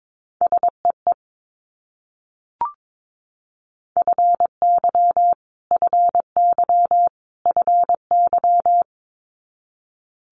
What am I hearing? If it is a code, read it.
Morse code, heard as a keyed tone, about 22 wpm, 700 Hz. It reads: SEE FY FY FY
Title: Multi-letter Phonemes Repeated 3x 22wpm